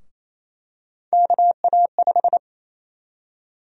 Morse code, heard as a keyed tone, about 28 wpm, 700 Hz. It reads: KA5
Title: USA Call Sign Prefixes - 28wpm